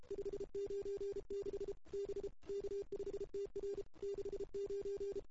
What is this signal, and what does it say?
Morse code, heard as a keyed tone, about 32 wpm, 400 Hz. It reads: 596 B K5TR 69
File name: CW / 2008-ss-cw-k5tr-op-n5rz -02:00